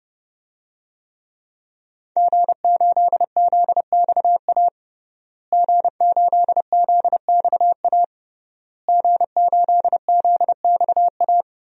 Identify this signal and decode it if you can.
Morse code, heard as a keyed tone, about 32 wpm, 700 Hz. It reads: G8ZXA G8ZXA G8ZXA
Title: DXCC-100 Call Signs - 5 Characters - Repeated 3x - 30wpm